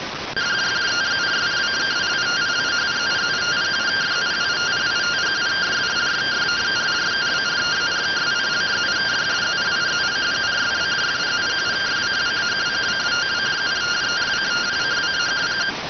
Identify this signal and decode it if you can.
Two-tone teleprinter signal, about 170 Hz shift.